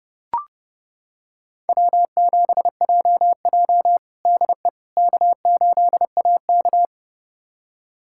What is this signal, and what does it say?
Morse code, heard as a keyed tone, about 30 wpm, 700 Hz. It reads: W7JJ DE K8AK
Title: USA Callsign DE Callsign - 4 Characters - 30wpm